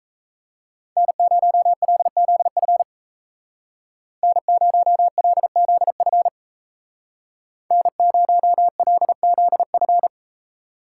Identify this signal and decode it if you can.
Morse code, heard as a keyed tone, about 36 wpm, 700 Hz. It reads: N0LZF N0LZF N0LZF